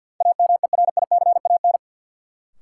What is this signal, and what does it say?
Morse code, heard as a keyed tone, about 47 wpm, 700 Hz. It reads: AMERICAN
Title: Sets of 2 Words - New Word Drill - 50wpm